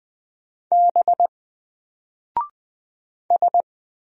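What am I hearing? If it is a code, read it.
Morse code, heard as a keyed tone, about 20 wpm, 700 Hz. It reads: B S
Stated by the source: Single letter and number Morse drill at 20 wpm, from N4LCD.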